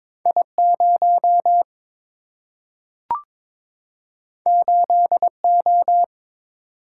Morse code, heard as a keyed tone, about 22 wpm, 700 Hz. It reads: I0 8O
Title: Single-Letter Single-Number 22wpm